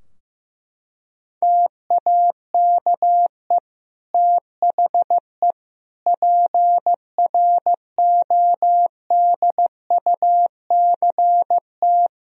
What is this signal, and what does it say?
Morse code, heard as a keyed tone, about 15 wpm, 700 Hz. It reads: TAKE THE PRODUCT
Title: Sentences from Top 1000 Words - Review Part 2 - 15wpm